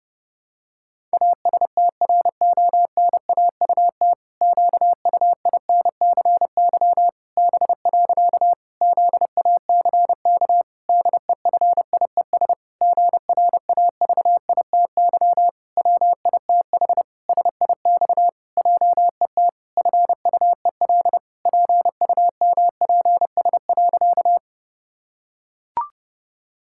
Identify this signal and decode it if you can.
Morse code, heard as a keyed tone, about 30 wpm, 700 Hz. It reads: ASTRONAUT QUINCY B. ZACK DEFIES GRAVITY WITH SIX JET FUEL PUMPS.
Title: Pangram Challenge v2 - 30wpm